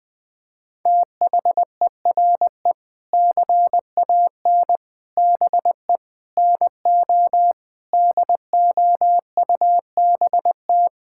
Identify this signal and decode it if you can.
Morse code, heard as a keyed tone, about 20 wpm, 700 Hz. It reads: THERE CAN BE NO DOUBT
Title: Sentences with 7-Word Phrases - 20wpm